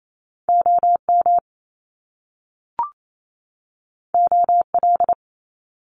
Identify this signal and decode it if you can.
Morse code, heard as a keyed tone, about 28 wpm, 700 Hz. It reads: OM OL
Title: Top Two Letter Combinations in Words 28wpm